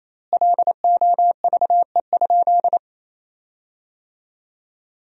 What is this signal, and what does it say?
Morse code, heard as a keyed tone, about 28 wpm, 700 Hz. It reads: LOVE?
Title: 4x Word-Spacing - Sentences from Top 400 Words - 28wpm